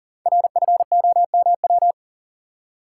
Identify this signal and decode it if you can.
Morse code, heard as a keyed tone, about 40 wpm, 700 Hz. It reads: RFOMW
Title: Code Groups - 5 Letters - 40wpm